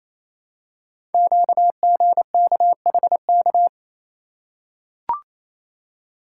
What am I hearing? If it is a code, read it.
Morse code, heard as a keyed tone, about 29 wpm, 700 Hz. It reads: QGKHK
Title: Code Groups - 5 Letters - 28wpm